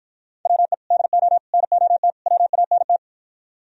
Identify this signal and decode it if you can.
Morse code, heard as a keyed tone, about 53 wpm, 700 Hz. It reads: WE DO NOT WANT